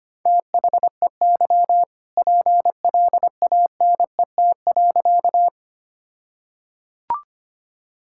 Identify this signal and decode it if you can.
Morse code, heard as a keyed tone, about 25 wpm, 700 Hz. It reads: THEY PLANET.
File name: practice / 25wpm / All-Age Appropriate Jokes - 25wpm